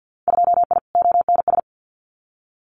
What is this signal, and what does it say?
Morse code, heard as a keyed tone, about 50 wpm, 700 Hz. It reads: 2I ONS
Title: Winter Field Day Exchanges Encore 50wpm